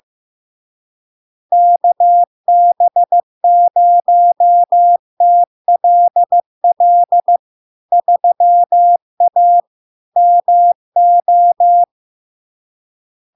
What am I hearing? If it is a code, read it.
Morse code, heard as a keyed tone, about 15 wpm, 700 Hz. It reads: KB0TLL 3A MO
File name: LARK-FD-2022-Log-in-CW-15wpm-700-Hz